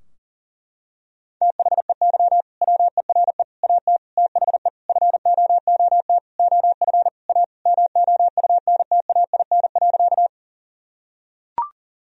Morse code, heard as a keyed tone, about 40 wpm, 700 Hz. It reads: THEY WERE AT THE FOOT OF A MOUNTAIN.
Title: Sentences from Top 1000 Words - Review Part 3 - 40wpm